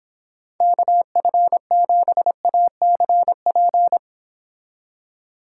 Morse code, heard as a keyed tone, about 26 wpm, 700 Hz. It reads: KF7ACP